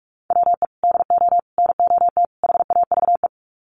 Morse code, heard as a keyed tone, about 44 wpm, 700 Hz. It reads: WE DO NOT HAVE
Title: Sets of 4 Words 45wpm